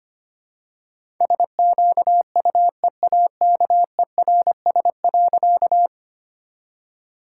Morse code, heard as a keyed tone, about 25 wpm, 700 Hz. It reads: SQUEAKERS.
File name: practice / 25wpm / All-Age Appropriate Jokes - 25wpm